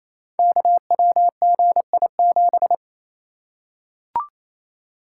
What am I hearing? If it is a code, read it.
Morse code, heard as a keyed tone, about 28 wpm, 700 Hz. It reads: KWGI7
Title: Code Groups - 5 Letter:Numbers - 28wpm